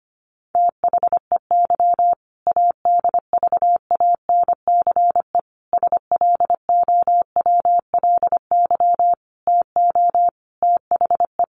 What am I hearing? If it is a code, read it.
Morse code, heard as a keyed tone, about 25 wpm, 700 Hz. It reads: THEY ADVANCE SLOWLY TO THE
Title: Sentences from Top 1400 Words 25wpm